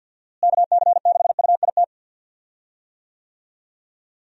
Morse code, heard as a keyed tone, about 46 wpm, 700 Hz. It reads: KC6UIT
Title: USA Call Signs - 6 Characters - Repeated 2x - 50wpm